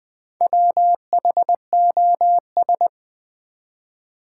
Morse code, heard as a keyed tone, about 20 wpm, 700 Hz. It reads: WHOS
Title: Build A Word - 1 Letter At A Time - 5-Letter Words - 20wpm